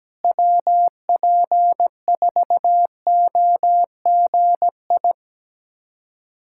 Morse code, heard as a keyed tone, about 17 wpm, 700 Hz. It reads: WP4OGI
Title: USA Callsign DE Callsign - 6 Characters - 17wpm